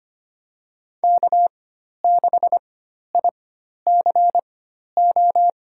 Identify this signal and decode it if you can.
Morse code, heard as a keyed tone, about 25 wpm, spaced as for 12 wpm, 700 Hz. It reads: K6ICO
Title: USA Call Signs - 5 Characters - 4x Character Spacing - 25wpm